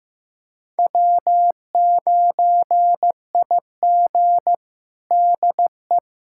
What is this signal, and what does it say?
Morse code, heard as a keyed tone, about 15 wpm, 700 Hz. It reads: W9IG DE
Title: USA Callsign DE Callsign - 4 Characters - 15wpm